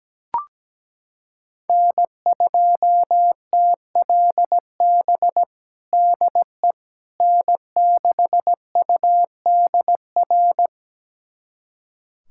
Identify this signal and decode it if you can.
Morse code, heard as a keyed tone, about 17 wpm, 700 Hz. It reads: N2TLB DE N6UDR
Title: USA Callsign DE Callsign - 5 Characters - 17wpm